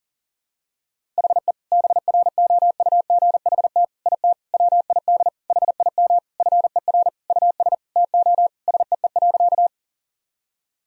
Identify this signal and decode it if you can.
Morse code, heard as a keyed tone, about 40 wpm, 700 Hz. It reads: HE BROUGHT IT WID HIM FER US TO SEE.
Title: Sentences from Top 500 Words and Abbreviations 40wpm